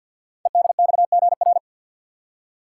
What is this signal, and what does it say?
Morse code, heard as a keyed tone, about 50 wpm, 690 Hz. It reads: EDXGR
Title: Code Groups - 5 Letters - 50wpm